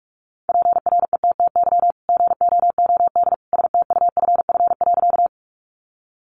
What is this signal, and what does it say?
Morse code, heard as a keyed tone, about 44 wpm, 700 Hz. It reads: PRETTY GOOD STUFF.
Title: Sentences from Top 1400 Words 45wpm